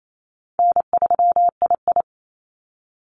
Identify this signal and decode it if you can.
Morse code, heard as a keyed tone, about 28 wpm, 700 Hz. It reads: N3II
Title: NA SOTA Activator Callsigns 28wpm